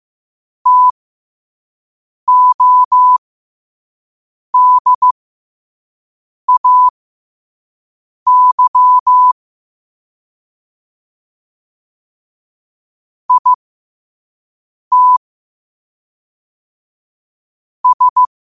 Morse code, heard as a keyed tone, about 15 wpm, spaced as for 5 wpm, 1000 Hz. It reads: TODAY IT S
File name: qotdfiles / 5wpm